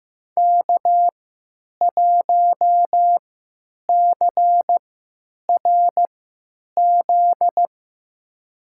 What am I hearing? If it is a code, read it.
Morse code, heard as a keyed tone, about 15 wpm, spaced as for 9 wpm, 700 Hz. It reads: K1CRZ
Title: USA Call Signs - 3x Character Spacing - 15wpm